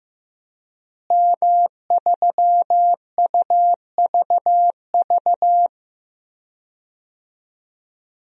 Morse code, heard as a keyed tone, about 14 wpm, 700 Hz. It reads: M3UVV
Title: DXCC-100 Call Signs - 5 Characters - Repeated 3x - 15wpm